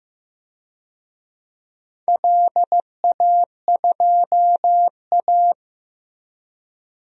Morse code, heard as a keyed tone, about 15 wpm, 700 Hz. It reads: LA2A
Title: DXCC-100 Call Signs - 4 Characters - Repeated 2x - 15wpm